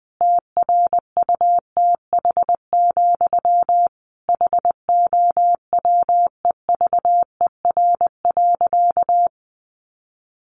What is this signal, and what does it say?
Morse code, heard as a keyed tone, about 20 wpm, 700 Hz. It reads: TRUTH, HOWEVER.
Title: Sentences from Top 800 Words 20wpm